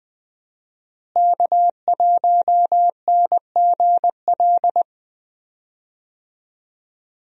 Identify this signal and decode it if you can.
Morse code, heard as a keyed tone, about 20 wpm, 700 Hz. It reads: K1NGL
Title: USA Call Signs - 5 Characters - Repeated 3x - 20wpm